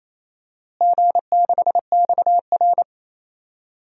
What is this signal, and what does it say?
Morse code, heard as a keyed tone, about 28 wpm, 700 Hz. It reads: G6XR
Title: DXCC-100 Call Signs - 4 Characters - Repeated 2x - 28wpm